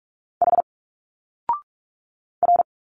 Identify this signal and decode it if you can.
Morse code, heard as a keyed tone, about 46 wpm, 700 Hz. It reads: H R
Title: Single Letters 45wpm